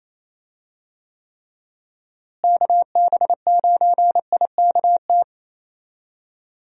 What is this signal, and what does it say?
Morse code, heard as a keyed tone, about 28 wpm, 700 Hz. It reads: KB9IKT
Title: USA Call Signs - 6 Characters - Repeated 3x - 28wpm